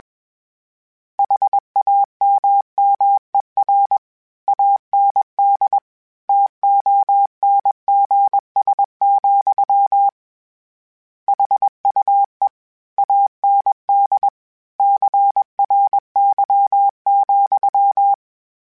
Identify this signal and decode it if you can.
Morse code, heard as a keyed tone, about 21 wpm, 800 Hz. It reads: HAMMER AND TONGS, HUE AND CRY,
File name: Misc / Misc-mp3s / Word Pairs @20